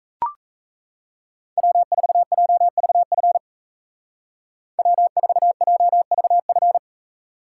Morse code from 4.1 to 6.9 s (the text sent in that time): W4JVF